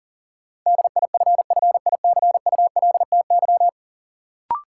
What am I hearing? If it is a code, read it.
Morse code, heard as a keyed tone, about 40 wpm, 700 Hz. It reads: DIFFICULTY
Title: Sentences from Top 1500 Words - New Word Drill - 40wpm